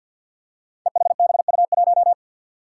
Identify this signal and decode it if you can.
Morse code, heard as a keyed tone, about 47 wpm, 700 Hz. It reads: EHBU1
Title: Five Challenging Letter-Number Combinations 50wpm